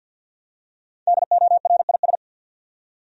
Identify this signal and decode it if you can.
Morse code, heard as a keyed tone, about 52 wpm, 700 Hz. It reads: DORIS
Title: USA Common Names - 50wpm